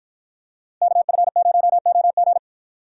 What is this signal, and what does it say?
Morse code, heard as a keyed tone, about 48 wpm, 700 Hz. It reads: KF0OG